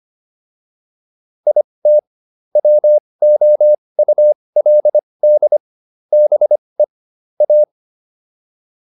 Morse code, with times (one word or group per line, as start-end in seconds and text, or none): 1.5-2.0 IT
2.6-5.6 WOULD
6.1-6.8 BE
7.4-7.6 A